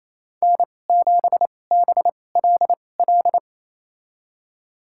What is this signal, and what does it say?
Morse code, heard as a keyed tone, about 28 wpm, spaced as for 20 wpm, 700 Hz. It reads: N7BLL
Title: USA Call Signs - 5 Characters - 2x Character Spacing - 28wpm